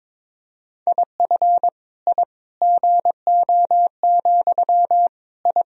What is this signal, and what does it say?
Morse code, heard as a keyed tone, about 21 wpm, 700 Hz. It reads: IF I GO, I
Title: Sentences from Top 1000 Words - Review Part 3 - 22wpm